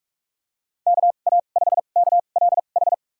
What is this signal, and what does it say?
Morse code, heard as a keyed tone, about 41 wpm, spaced as for 31 wpm, 700 Hz. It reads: KA5KLH
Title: USA Call Signs - 6 Characters - 2x Character Spacing - 45wpm